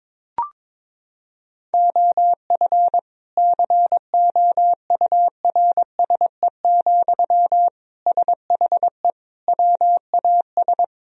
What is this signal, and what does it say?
Morse code, heard as a keyed tone, about 22 wpm, 700 Hz. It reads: OF COURSE, SHE WAS
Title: Sentences from Top 1000 Words - Review Part 4 - 22wpm